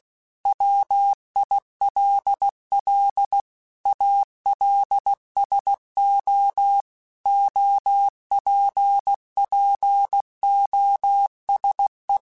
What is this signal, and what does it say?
Morse code, heard as a keyed tone, about 16 wpm, 800 Hz. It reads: WILL ALSO OPPOSE
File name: Technical Subjects / Technical Subjects-mp3s / XC @15